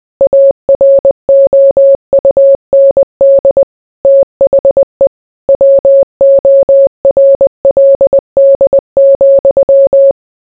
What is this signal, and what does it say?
Morse code, heard as a keyed tone, about 20 wpm, 550 Hz. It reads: AROUND THE WORLD,